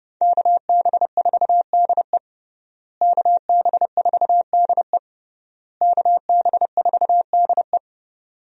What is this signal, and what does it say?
Morse code, heard as a keyed tone, about 32 wpm, 700 Hz. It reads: KB4DE KB4DE KB4DE
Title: USA Call Signs - 5 Characters - Repeated 3x - 30wpm